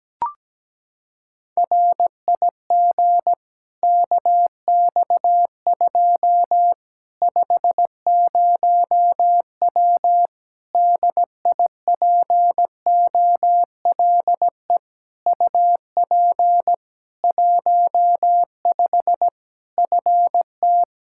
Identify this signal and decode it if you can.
Morse code, heard as a keyed tone, about 17 wpm, 700 Hz. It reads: RIG KX2 50W DIPOLE UP 15 FT
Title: K1USN SST QSO Practice 17wpm